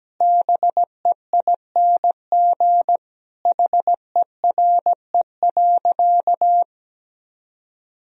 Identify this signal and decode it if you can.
Morse code, heard as a keyed tone, about 17 wpm, 700 Hz. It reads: BEING HERE.
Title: Sentences from Top 1000 Words - Review Part 3 - 17wpm